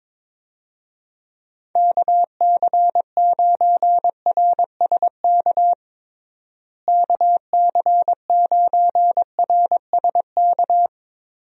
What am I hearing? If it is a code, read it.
Morse code, heard as a keyed tone, about 22 wpm, 700 Hz. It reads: KC9RSK KC9RSK
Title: USA Call Signs - 6 Characters - Repeated 2x - 22wpm